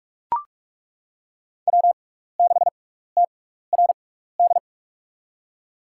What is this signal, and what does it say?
Morse code, heard as a keyed tone, about 41 wpm, spaced as for 15 wpm, 690 Hz. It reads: W6TRD
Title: USA Call Signs - 5 Characters - 6x Character Spacing - 45wpm